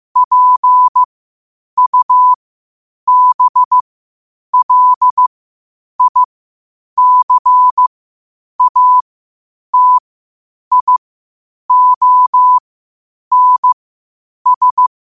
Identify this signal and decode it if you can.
Morse code, heard as a keyed tone, about 15 wpm, spaced as for 8 wpm, 1000 Hz. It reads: PUBLICATIONS
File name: qotdfiles / 8wpm